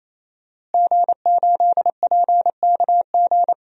Morse code, heard as a keyed tone, about 28 wpm, 700 Hz. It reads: G8PKG
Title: DXCC-100 Call Signs - 5 Characters - Repeated 3x - 28wpm